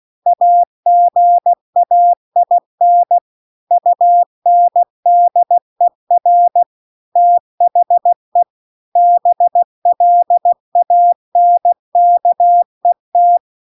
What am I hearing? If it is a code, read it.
Morse code, heard as a keyed tone, about 16 wpm, 700 Hz. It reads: AGAIN UNDER THE BLANKET